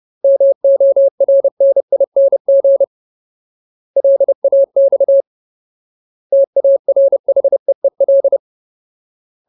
Morse code, heard as a keyed tone, about 30 wpm, 550 Hz. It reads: MORNING LAX TARHEEL
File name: Ultimate Word List / 30WPM Ultimate